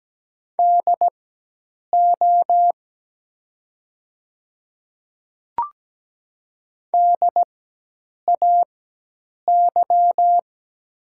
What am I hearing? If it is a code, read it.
Morse code, heard as a keyed tone, about 17 wpm, spaced as for 8 wpm, 700 Hz. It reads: DO DAY
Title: 4x Character Spacing - Top 200 Words - 17wpm